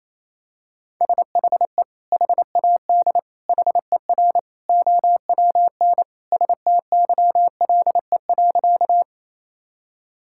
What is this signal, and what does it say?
Morse code, heard as a keyed tone, about 28 wpm, 700 Hz. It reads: SHE HAD HER OWN STYLE.
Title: Sentences from Top 1000 Words - Review Part 5 - 28wpm